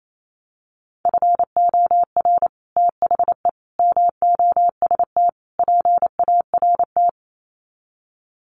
Morse code, heard as a keyed tone, about 28 wpm, 700 Hz. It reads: FOR THE MOST PART
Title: Sets of 4 Words 28wpm